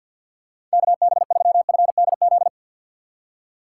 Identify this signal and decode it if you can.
Morse code, heard as a keyed tone, about 51 wpm, 700 Hz. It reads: KB3FDZ
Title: USA Call Signs - 6 Characters - 4x Character Spacing - 50wpm